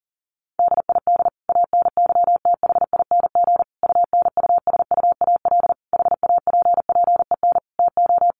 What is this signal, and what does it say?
Morse code, heard as a keyed tone, about 40 wpm, 700 Hz. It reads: DID ANYTHING UNUSUAL HAPPEN TO
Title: Sentences from Top 1400 Words 40wpm